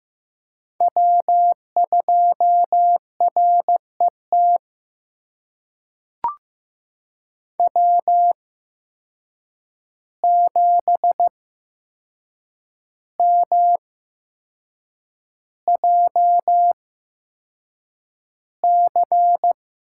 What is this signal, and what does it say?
Morse code, heard as a keyed tone, about 15 wpm, spaced as for 6 wpm, 700 Hz. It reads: W2RET W7MJC
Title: USA Call Signs - 5 Characters - 8x Character Spacing - 15wpm